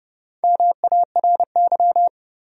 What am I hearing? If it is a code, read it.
Morse code, heard as a keyed tone, about 31 wpm, 700 Hz. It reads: MARY
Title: USA Common Names - 30wpm